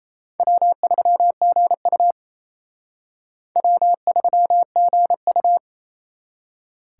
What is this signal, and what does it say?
Morse code, heard as a keyed tone, about 34 wpm, 700 Hz. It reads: W3GU W3GU
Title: Speed Racing - USA 1x2 Call Signs 28wpm